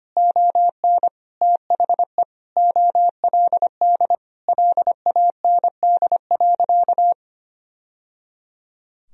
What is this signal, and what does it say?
Morse code, heard as a keyed tone, about 25 wpm, 700 Hz. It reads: ON THE OLD LAND.
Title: Sentences from Top 1000 Words - Review Part 1 - 25wpm